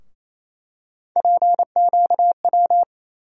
Morse code, heard as a keyed tone, about 27 wpm, 700 Hz. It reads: PQW
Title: USA Call Sign Postfixes - 3 Letters 28wpm